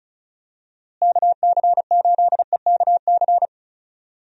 Morse code, heard as a keyed tone, about 36 wpm, 700 Hz. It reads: KC8EKC
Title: USA Call Signs - 6 Characters - Repeated 2x - 35wpm